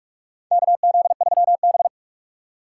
Morse code, heard as a keyed tone, about 47 wpm, 700 Hz. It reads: KZ3B
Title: USA Call Signs - 4 Characters - 4x Character Spacing - 45wpm